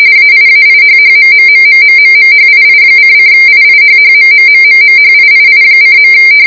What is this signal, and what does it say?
RTTY, about 45.45 baud. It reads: AIAIHHMMMMMMMHII::IIHHMMMMMHIII:.